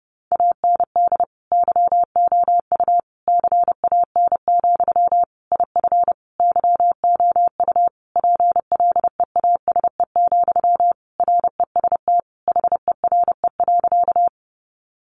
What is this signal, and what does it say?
Morse code, heard as a keyed tone, about 30 wpm, 700 Hz. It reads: AND YOU CAN, IF YOU PLEASE, REST HERE.
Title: Sentences from Top 1500 Words 30wpm